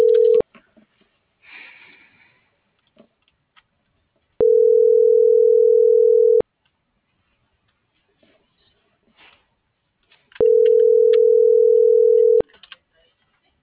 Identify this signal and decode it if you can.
Ringback tone (North America), 440 + 480 Hz.